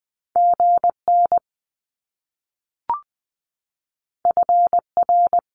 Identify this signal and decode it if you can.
Morse code, heard as a keyed tone, about 20 wpm, 700 Hz. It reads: G N F R
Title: Top Two Letter Combinations in Words 20wpm